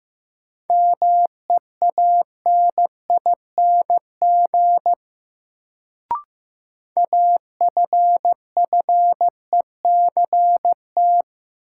Morse code, heard as a keyed tone, about 15 wpm, 700 Hz. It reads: MEANING AFFECT
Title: Sentences from Top 900 Words - New Word Drill - 15wpm